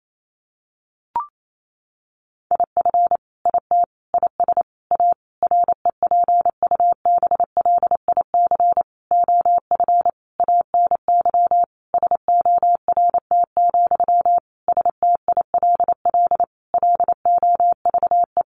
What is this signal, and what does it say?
Morse code, heard as a keyed tone, about 28 wpm, 700 Hz. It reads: IF IT IS A REPUBLIC OF ANY SORT, STILL LOVE
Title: Sentences from Top 1500 Words 28wpm